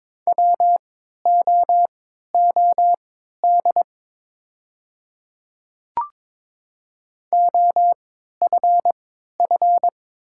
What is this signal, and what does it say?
Morse code, heard as a keyed tone, about 22 wpm, spaced as for 12 wpm, 700 Hz. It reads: WOOD OFF
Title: 3x Character Spacing - Top 300 Words - 22wpm